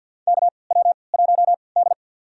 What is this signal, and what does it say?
Morse code, heard as a keyed tone, about 45 wpm, spaced as for 27 wpm, 700 Hz. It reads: KW1D
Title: USA Call Signs - 4 Characters - 3x Character Spacing - 50wpm